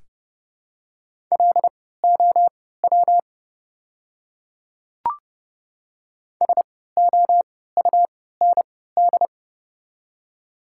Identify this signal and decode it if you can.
Morse code, heard as a keyed tone, about 30 wpm, spaced as for 17 wpm, 700 Hz. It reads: LOW SOUND
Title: 3x Character Spacing - Top 200 Words - 30wpm